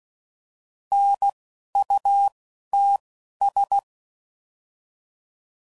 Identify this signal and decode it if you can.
Morse code, heard as a keyed tone, about 16 wpm, spaced as for 11 wpm, 800 Hz. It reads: NUTS
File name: Prefixes / By First Letter / Ns